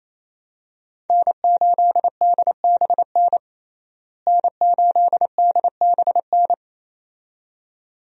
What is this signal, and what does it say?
Morse code, heard as a keyed tone, about 28 wpm, 700 Hz. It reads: N8DBN N8DBN
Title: USA Call Signs - 5 Characters - Repeated 2x - 28wpm